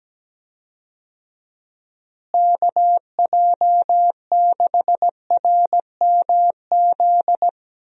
Morse code, heard as a keyed tone, about 17 wpm, 700 Hz. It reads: KJ6RMZ
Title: USA Call Signs - 6 Characters - Repeated 3x - 17wpm